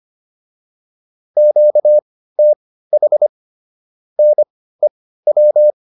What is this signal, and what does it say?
Morse code, heard as a keyed tone, about 25 wpm, spaced as for 15 wpm, 600 Hz. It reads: QTH NEW